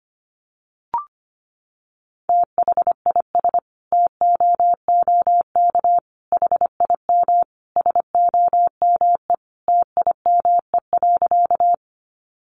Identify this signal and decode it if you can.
Morse code, heard as a keyed tone, about 25 wpm, 700 Hz. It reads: THIS TOOK HIM SOME TIME.
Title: Sentences from Top 300 Words 25wpm